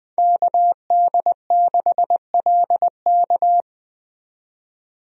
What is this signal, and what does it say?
Morse code, heard as a keyed tone, about 21 wpm, 700 Hz. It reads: KD6LK
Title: USA Call Signs - 5 Characters - 20wpm